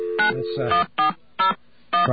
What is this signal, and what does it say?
DTMF, heard as telephone touch tones, tones of about 130 ms, with speech. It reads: 91401